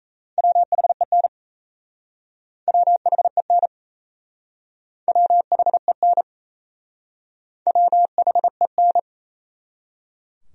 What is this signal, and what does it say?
Morse code, heard as a keyed tone, about 37 wpm, 700 Hz. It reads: WHEN WHEN WHEN WHEN